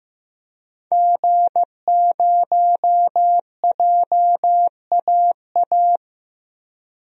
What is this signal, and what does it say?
Morse code, heard as a keyed tone, about 15 wpm, 700 Hz. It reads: G0JAA
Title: DXCC-100 Call Signs - 5 Characters - 15wpm